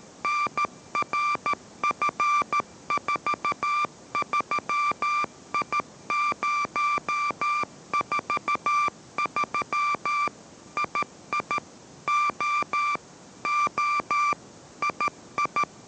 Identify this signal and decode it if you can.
Morse code, heard as a keyed tone, about 13 wpm, 1180 Hz. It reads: NRF43I043 II O O II